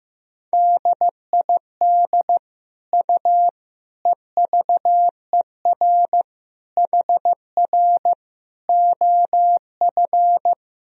Morse code, heard as a keyed tone, about 16 wpm, 700 Hz. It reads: DID U EVER HR OF
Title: Sentences from Top 400 Words and Abbreviations 15wpm